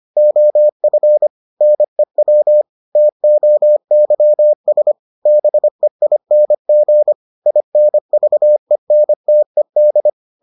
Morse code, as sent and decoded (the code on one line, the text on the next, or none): --- ..-. / -. .-- / - --- -.-- ... / -... .. -. --. / .. -. ...- . -. - . -..
OF NEW TOYS BEING INVENTED